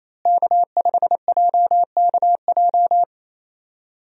Morse code, heard as a keyed tone, about 28 wpm, 700 Hz. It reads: K5JKJ
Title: USA Call Signs - 5 Characters - Repeated 3x - 28wpm